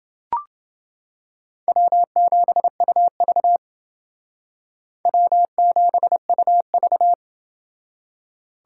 Morse code, heard as a keyed tone, about 28 wpm, 700 Hz. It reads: W7UV W7UV